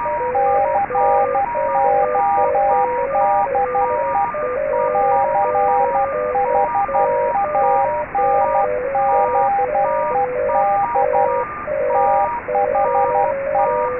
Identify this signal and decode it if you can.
Morse code, heard as a keyed tone, about 12 wpm, 800 Hz, with radio noise. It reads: CODE CHANGED THE